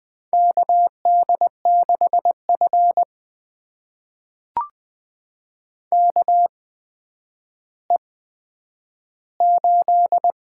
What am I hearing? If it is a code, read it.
Morse code, heard as a keyed tone, about 20 wpm, spaced as for 10 wpm, 700 Hz. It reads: KD6F KE8
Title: USA Call Signs - 8x Character Spacing - 20wpm